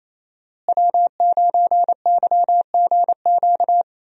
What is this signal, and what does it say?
Morse code, heard as a keyed tone, about 28 wpm, 700 Hz. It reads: W9YGQ
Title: USA Call Signs - 5 Characters - Repeated 3x - 28wpm